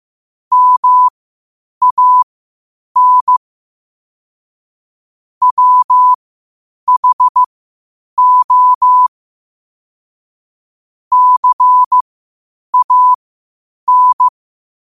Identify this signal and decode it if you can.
Morse code, heard as a keyed tone, about 15 wpm, spaced as for 8 wpm, 1000 Hz. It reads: MAN WHO CAN
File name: qotdfiles / 8wpm